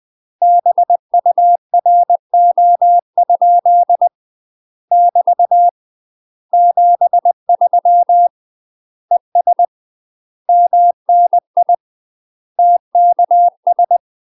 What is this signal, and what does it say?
Morse code, heard as a keyed tone, about 20 wpm, 700 Hz. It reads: BURO? <BT> 73 ES MNI TKS
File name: UK MASTER QSOs TRAINING / 20WPM / UK MASTER QSOs 0052